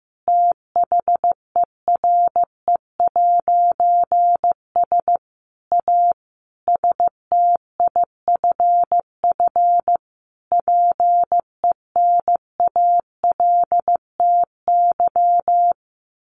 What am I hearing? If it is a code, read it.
Morse code, heard as a keyed tone, about 15 wpm, 700 Hz. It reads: THERE'S A STIFF PENALTY
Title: A Sound Of Thunder 15wpm